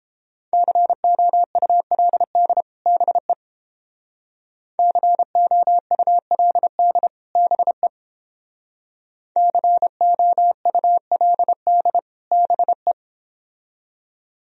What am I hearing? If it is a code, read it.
Morse code, heard as a keyed tone, about 31 wpm, 700 Hz. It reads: COULD BE COULD BE COULD BE